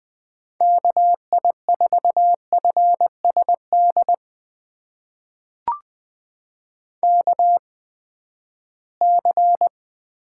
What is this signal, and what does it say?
Morse code, heard as a keyed tone, about 20 wpm, spaced as for 14 wpm, 700 Hz. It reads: KI4FSD KC
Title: USA Call Signs - 6 Characters - 8x Character Spacing - 20wpm